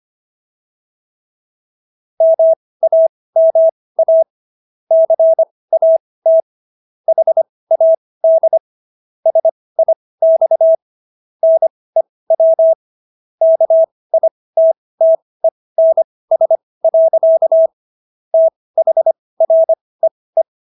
Morse code, heard as a keyed tone, about 25 wpm, spaced as for 18 wpm, 650 Hz. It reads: MAMA CAT HAD SIX NEW KITTENS. THREE